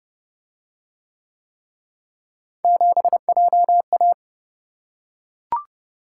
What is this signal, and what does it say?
Morse code, heard as a keyed tone, about 27 wpm, 700 Hz. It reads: 7JA